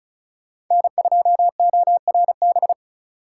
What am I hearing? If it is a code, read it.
Morse code, heard as a keyed tone, about 44 wpm, 700 Hz. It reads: N2ORB